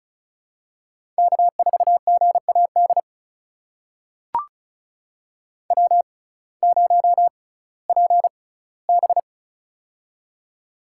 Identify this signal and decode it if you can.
Morse code, heard as a keyed tone, about 34 wpm, spaced as for 19 wpm, 700 Hz. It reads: K4GAD W0PB
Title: USA Call Signs - 6x Character Spacing - 35wpm